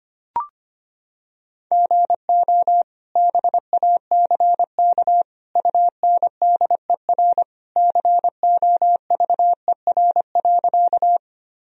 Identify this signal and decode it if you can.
Morse code, heard as a keyed tone, about 25 wpm, 700 Hz. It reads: GO BACK UNDER COVER.
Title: Sentences from Top 1000 Words - Review Part 1 - 25wpm